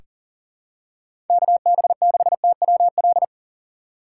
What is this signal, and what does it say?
Morse code, heard as a keyed tone, about 40 wpm, 700 Hz. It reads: KB6TWL
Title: USA Call Signs - 2x Character Spacing - 40wpm